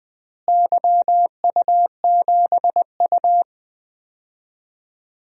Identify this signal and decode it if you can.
Morse code, heard as a keyed tone, about 20 wpm, 700 Hz. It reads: YU7U
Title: DXCC-100 Call Signs - 4 Characters - Repeated 3x - 20wpm